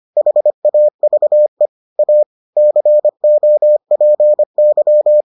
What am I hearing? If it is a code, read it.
Morse code, heard as a keyed tone, about 25 wpm, 600 Hz. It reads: HAVE A COPY